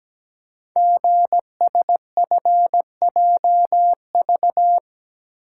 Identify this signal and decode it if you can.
Morse code, heard as a keyed tone, about 17 wpm, 700 Hz. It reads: GSFJV
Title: Code Groups - 5 Letters - 17wpm